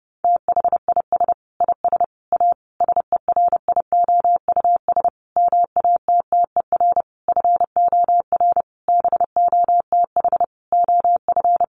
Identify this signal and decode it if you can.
Morse code, heard as a keyed tone, about 30 wpm, 700 Hz. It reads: THIS IS A SERIOUS MATTER FOR BOTH OF
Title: Sentences from Top 1100 Words 30wpm